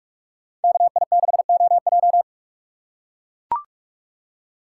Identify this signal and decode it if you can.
Morse code, heard as a keyed tone, about 41 wpm, 700 Hz. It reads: KI6OJ
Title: USA Call Signs - 5 Characters - 4x Character Spacing - 45wpm